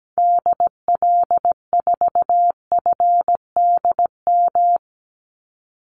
Morse code, heard as a keyed tone, about 17 wpm, 700 Hz. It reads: DL4FDM
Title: Call Signs Repeated 2x - 17wpm